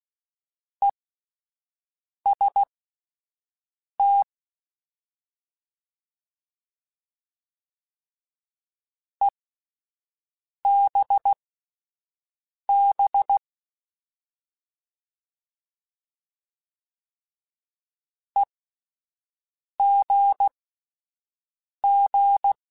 Morse code, heard as a keyed tone, about 16 wpm, spaced as for 5 wpm, 800 Hz. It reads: EST EBB EGG
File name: Alphabetical / Alphabetical mp3s / Es @5